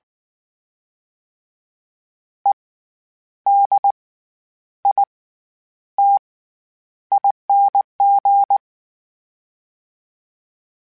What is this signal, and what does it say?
Morse code, heard as a keyed tone, about 19 wpm, spaced as for 9 wpm, 800 Hz. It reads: EDITING
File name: Special INGs / Special INGs-mp3s / SlowToFaster ing @18